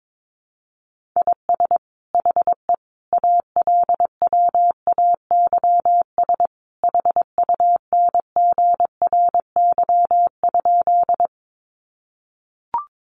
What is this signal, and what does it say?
Morse code, heard as a keyed tone, about 22 wpm, 700 Hz. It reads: IS HE ALWAYS HUNGRY?
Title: Sentences from Top 1400 Words 22wpm